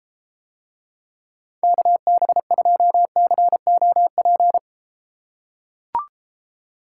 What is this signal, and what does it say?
Morse code, heard as a keyed tone, about 34 wpm, 700 Hz. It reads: KB2COP